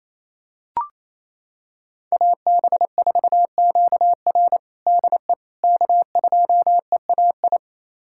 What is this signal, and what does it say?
Morse code, heard as a keyed tone, about 28 wpm, 700 Hz. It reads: AB4QR DE K2EAI
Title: USA Callsign DE Callsign - 5 Characters - 28wpm